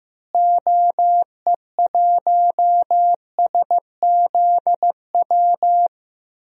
Morse code, heard as a keyed tone, about 15 wpm, 700 Hz. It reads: OE1SZW
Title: DXCC-100 Call Signs - 6 Characters - Repeated 3x - 15wpm